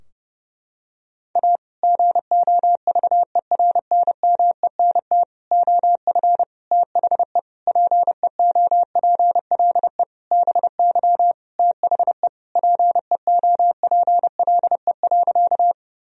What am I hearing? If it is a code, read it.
Morse code, heard as a keyed tone, about 30 wpm, 700 Hz. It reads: A GOVERNMENT OF THE PEOPLE BY THE PEOPLE.
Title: Sentences from Top 1000 Words - Review Part 3 - 30wpm